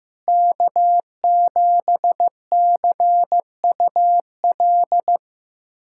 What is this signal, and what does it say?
Morse code, heard as a keyed tone, about 15 wpm, 700 Hz. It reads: K7CUL
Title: USA Callsign DE Callsign 15wpm